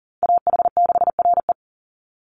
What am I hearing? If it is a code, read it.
Morse code, heard as a keyed tone, about 40 wpm, 700 Hz. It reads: AH6RE
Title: Call Signs - 40wpm